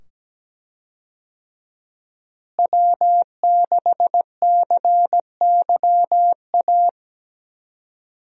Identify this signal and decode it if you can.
Morse code, heard as a keyed tone, about 17 wpm, 700 Hz. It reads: W6CYA
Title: USA Call Signs - 5 Characters - Repeated 2x - 17wpm